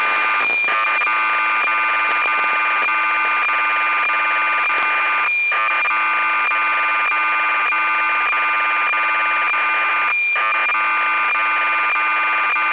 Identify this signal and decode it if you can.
Cassette data (Kansas City type).
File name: uo-11-trimmed